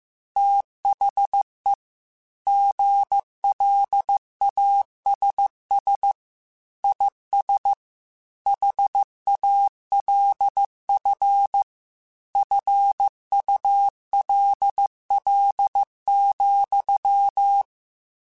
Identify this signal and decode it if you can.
Morse code, heard as a keyed tone, about 15 wpm, 800 Hz. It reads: THE GLASS IS HALF FULL,
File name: Chuckles / Chuckles-mp3s / Glass Half Full @13